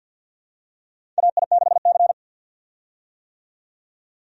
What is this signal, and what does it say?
Morse code, heard as a keyed tone, about 47 wpm, 700 Hz. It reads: AI6C
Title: USA Call Signs - 4 Characters - 50wpm